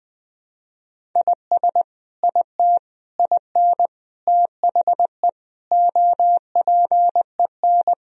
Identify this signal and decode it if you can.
Morse code, heard as a keyed tone, about 20 wpm, 700 Hz. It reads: IS IT IN THE OPEN
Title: Sentences from Top 1000 Words - Review Part 1 - 20wpm